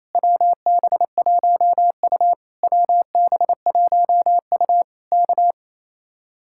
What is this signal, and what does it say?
Morse code, heard as a keyed tone, about 27 wpm, 700 Hz. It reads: WB1U WB1U K
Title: Speed-Racing Common QSO Phrases - 28wpm